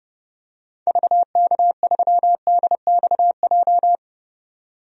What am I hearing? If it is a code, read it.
Morse code, heard as a keyed tone, about 30 wpm, 700 Hz. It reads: VK3DXJ
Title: DXCC-100 Call Signs - 6 Characters - 30wpm